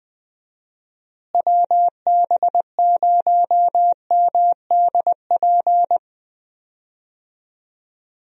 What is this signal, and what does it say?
Morse code, heard as a keyed tone, about 20 wpm, 700 Hz. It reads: WB0MDP
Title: USA Call Signs - 6 Characters - Repeated 3x - 20wpm